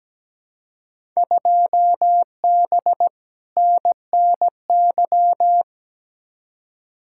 Morse code, heard as a keyed tone, about 17 wpm, 700 Hz. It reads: 2B NNY
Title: Field Day Exchanges Encore - 17wpm